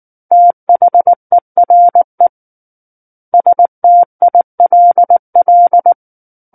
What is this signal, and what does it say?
Morse code, heard as a keyed tone, about 19 wpm, 700 Hz. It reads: THERE STILL
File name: Book One / 10-Chapter 10 - 19 WPM